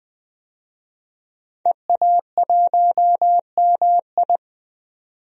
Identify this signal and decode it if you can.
Morse code, heard as a keyed tone, about 21 wpm, 700 Hz. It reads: EA1MI
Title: DXCC-100 Call Signs - 5 Characters - 20wpm